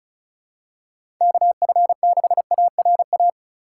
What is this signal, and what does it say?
Morse code, heard as a keyed tone, about 36 wpm, 700 Hz. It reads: KF6ARA
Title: USA Call Signs - 6 Characters - 2x Character Spacing - 35wpm